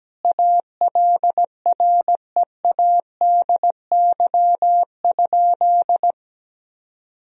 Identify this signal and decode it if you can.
Morse code, heard as a keyed tone, about 17 wpm, 700 Hz. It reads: ALREADY?
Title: Sentences from Top 1000 Words - Review Part 3 - 17wpm